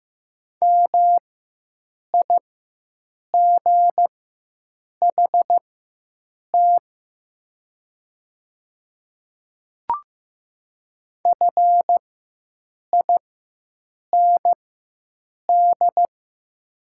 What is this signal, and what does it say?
Morse code, heard as a keyed tone, about 15 wpm, spaced as for 7 wpm, 700 Hz. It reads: MIGHT FIND
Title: 4x Character Spacing - Top 300 Words - 15wpm